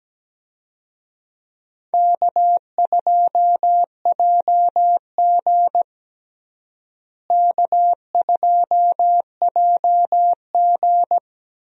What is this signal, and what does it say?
Morse code, heard as a keyed tone, about 17 wpm, 700 Hz. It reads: K2JG K2JG
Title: USA Call Signs - 4 Characters - Repeated 2x - 17wpm